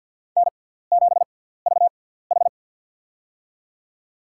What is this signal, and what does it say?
Morse code, heard as a keyed tone, about 46 wpm, spaced as for 17 wpm, 700 Hz. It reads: N7VH
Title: USA Call Signs - 6x Character Spacing - 50wpm